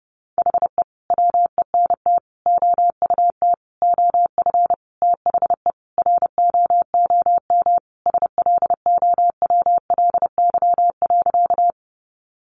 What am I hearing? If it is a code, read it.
Morse code, heard as a keyed tone, about 30 wpm, 700 Hz. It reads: HE WENT OUT OF THE ROOM SLOWLY.
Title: Sentences from Top 1100 Words 30wpm